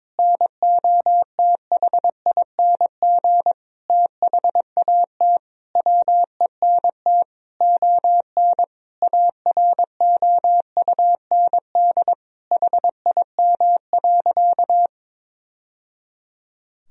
Morse code, heard as a keyed tone, about 22 wpm, 700 Hz. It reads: NOTHING THAT WENT ON AROUND HIM.
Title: Sentences from Top 1000 Words - Review Part 3 - 22wpm